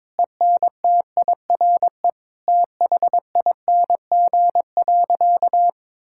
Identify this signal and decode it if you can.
Morse code, heard as a keyed tone, about 22 wpm, 700 Hz. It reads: ENTIRE THING.
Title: Sentences from Top 1000 Words - Review Part 5 - 22wpm